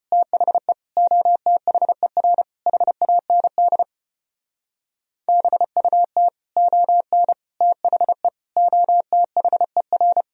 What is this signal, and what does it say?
Morse code, heard as a keyed tone, about 33 wpm, 700 Hz. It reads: THE OTHER HAND BUT ON THE OTHER